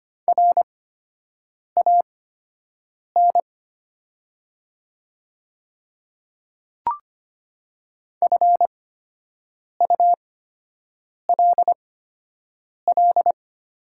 Morse code, heard as a keyed tone, about 25 wpm, spaced as for 7 wpm, 700 Hz. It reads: RAN FULL
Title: 8x Character Spacing - Top 500 Words - 25wpm